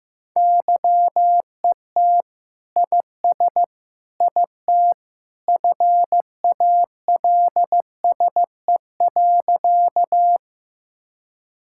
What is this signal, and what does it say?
Morse code, heard as a keyed tone, about 15 wpm, 700 Hz. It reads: YET IS IT FALSE.
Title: Sentences from Top 1600 Words 15wpm